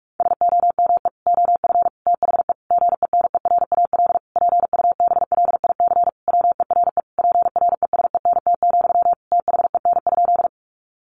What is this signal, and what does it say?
Morse code, heard as a keyed tone, about 44 wpm, 700 Hz. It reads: SOME OF THE GENERAL PUBLIC WERE PRESENT, THEN?
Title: Sentences from Top 800 Words 45wpm